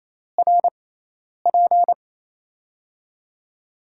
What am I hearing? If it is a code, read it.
Morse code, heard as a keyed tone, about 28 wpm, spaced as for 10 wpm, 700 Hz. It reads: RP
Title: USA Call Signs - 6 Characters - 6x Character Spacing - 28wpm